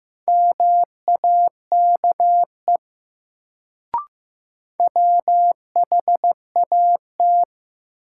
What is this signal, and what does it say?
Morse code, heard as a keyed tone, about 15 wpm, 700 Hz. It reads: MAKE WHAT
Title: Sets of 2 Words - New Word Drill - 15wpm